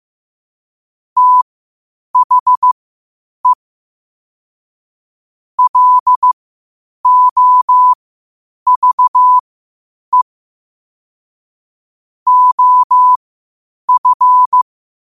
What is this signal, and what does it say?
Morse code, heard as a keyed tone, about 15 wpm, spaced as for 8 wpm, 1000 Hz. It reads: THE LOVE OF
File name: qotdfiles / 8wpm